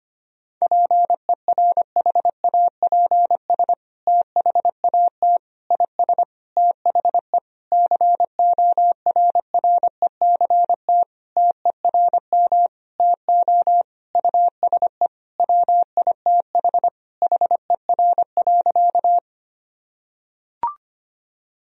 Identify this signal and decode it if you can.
Morse code, heard as a keyed tone, about 25 wpm, 700 Hz. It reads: PERHAPS THAT IS THE CORRECT TERM TO USE WITH HER.
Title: Sentences from Top 1000 Words - Review Part 4 - 25wpm